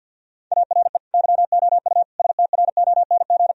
Morse code, heard as a keyed tone, about 50 wpm, 700 Hz. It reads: ARE YOU STRONG
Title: Sentences from Top 1000 Words - Review Part 3 - 50wpm